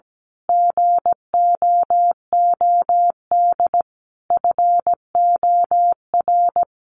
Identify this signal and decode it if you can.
Morse code, heard as a keyed tone, about 17 wpm, 700 Hz. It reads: GOOD FOR
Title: Sentences from Top 1400 Words 17wpm